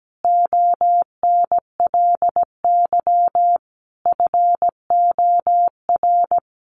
Morse code, heard as a keyed tone, about 17 wpm, 700 Hz. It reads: ONLY FOR
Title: Sentences from Top 1000 Words 17wpm